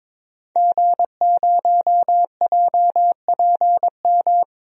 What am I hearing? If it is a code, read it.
Morse code, heard as a keyed tone, about 22 wpm, 700 Hz. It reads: G0JPM
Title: DXCC-100 Call Signs - 5 Characters - 22wpm